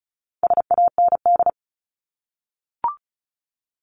Morse code, heard as a keyed tone, about 34 wpm, 700 Hz. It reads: SAND
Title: Top 1501-1750 Words 35wpm